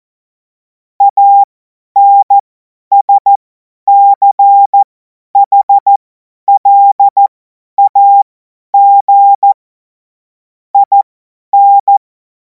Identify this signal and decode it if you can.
Morse code, heard as a keyed tone, about 14 wpm, spaced as for 10 wpm, 800 Hz. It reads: ANSCHLAG IN
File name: Morsekurs / Witze / Witz 03. März 2026 50 F70